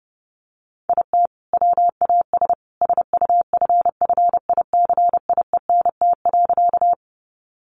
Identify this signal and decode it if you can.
Morse code, heard as a keyed tone, about 30 wpm, 700 Hz. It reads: IT WAS SUFFICIENT.
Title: Sentences from Top 1400 Words 30wpm